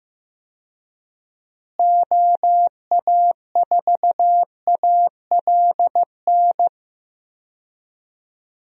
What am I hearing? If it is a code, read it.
Morse code, heard as a keyed tone, about 15 wpm, 700 Hz. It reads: OA4ALN
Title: USA Call Signs - 6 Characters - Repeated 2x - 15wpm